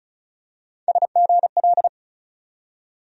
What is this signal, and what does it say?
Morse code, heard as a keyed tone, about 36 wpm, 700 Hz. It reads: SGL
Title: USA Call Sign Postfixes - 3 Letters 35wpm